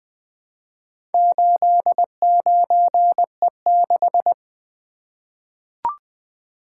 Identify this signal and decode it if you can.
Morse code, heard as a keyed tone, about 20 wpm, 700 Hz. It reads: 89E6
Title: Four Letter-Number Combinations - 20wpm